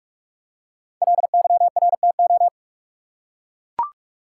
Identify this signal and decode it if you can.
Morse code, heard as a keyed tone, about 44 wpm, 700 Hz. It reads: LYRTO
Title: Code Groups - 5 Letters - 45wpm